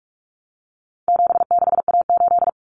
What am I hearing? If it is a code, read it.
Morse code, heard as a keyed tone, about 44 wpm, 700 Hz. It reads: 76A8
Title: Common Cut Numbers - 4-Digit 45wpm